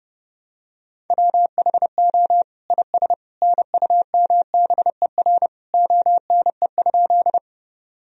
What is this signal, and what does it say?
Morse code, heard as a keyed tone, about 30 wpm, 700 Hz. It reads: WHO IS NUMBER ONE?
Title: Sentences from Top 1000 Words - Review Part 1 - 30wpm